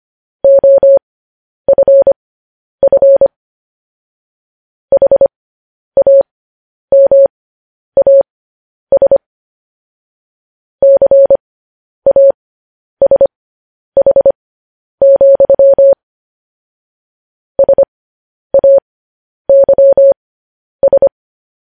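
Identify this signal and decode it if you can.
Morse code, heard as a keyed tone, about 25 wpm, spaced as for 10 wpm, 550 Hz. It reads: OFF HAMAS CASH, SAYS
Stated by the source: Morse code news headlines.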